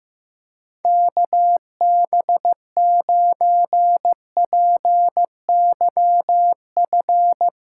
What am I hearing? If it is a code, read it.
Morse code, heard as a keyed tone, about 16 wpm, 700 Hz. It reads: KB9PYF
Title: USA Call Signs - 6 Characters - 4x Character Spacing - 15wpm